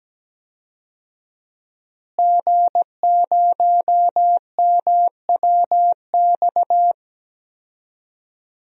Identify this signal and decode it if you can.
Morse code, heard as a keyed tone, about 17 wpm, 700 Hz. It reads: G0MWX
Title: DXCC-100 Call Signs - 5 Characters - Repeated 2x - 17wpm